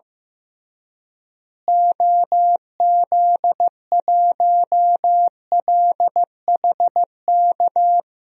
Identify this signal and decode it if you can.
Morse code, heard as a keyed tone, about 15 wpm, 700 Hz. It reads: OZ1LHK
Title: DXCC-100 Call Signs - 6 Characters - Repeated 2x - 15wpm